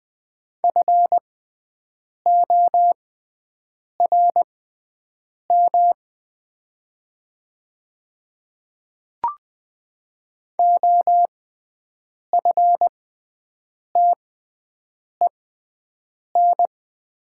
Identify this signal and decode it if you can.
Morse code, heard as a keyed tone, about 20 wpm, spaced as for 7 wpm, 700 Hz. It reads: FORM OFTEN
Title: 6x Character Spacing - Top 300 Words - 20wpm